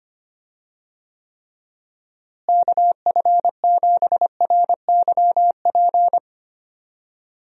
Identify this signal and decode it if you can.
Morse code, heard as a keyed tone, about 25 wpm, 700 Hz. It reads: KF7RYP
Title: USA Call Signs - 6 Characters - Repeated 3x - 25wpm